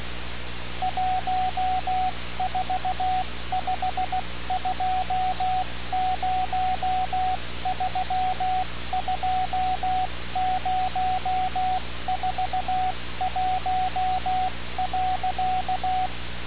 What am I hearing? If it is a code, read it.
Morse code, heard as a keyed tone, about 16 wpm, 740 Hz, with radio noise. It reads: 1452032041.
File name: radiocap / REC z